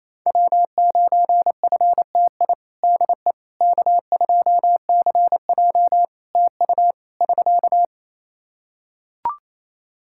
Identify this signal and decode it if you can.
Morse code, heard as a keyed tone, about 28 wpm, 700 Hz. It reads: W9FTI DE K2CJ TU <SK>
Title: K1USN SST QSO Practice 28wpm